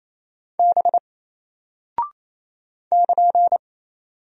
Morse code, heard as a keyed tone, about 28 wpm, 700 Hz. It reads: B <KN>
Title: Single Character (Letter:Number:Punctuation:Prosign) Alt 28wpm